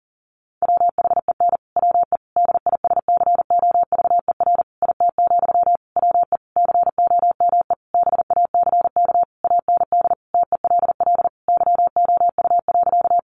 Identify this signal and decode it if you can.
Morse code, heard as a keyed tone, about 40 wpm, 700 Hz. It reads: WHEN WE DISCOVER IT, WE COME BACK AND TELL YOU.
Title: Sentences from Top 1100 Words 40wpm